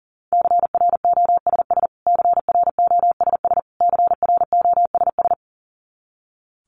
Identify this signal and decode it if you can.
Morse code, heard as a keyed tone, about 38 wpm, 700 Hz. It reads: CROSS CROSS CROSS
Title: Top 300 Words Repeated 40wpm